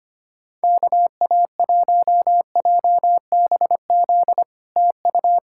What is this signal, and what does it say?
Morse code, heard as a keyed tone, about 25 wpm, 700 Hz. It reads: KA1JBZ TU
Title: SOTA QSO Practice 25wpm